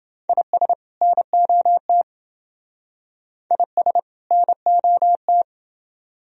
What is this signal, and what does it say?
Morse code, heard as a keyed tone, about 28 wpm, 700 Hz. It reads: IS NOT IS NOT